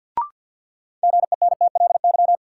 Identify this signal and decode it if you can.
Morse code, heard as a keyed tone, about 53 wpm, 700 Hz. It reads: GENTLY
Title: Sentences from Top 1500 Words - New Word Drill - 50wpm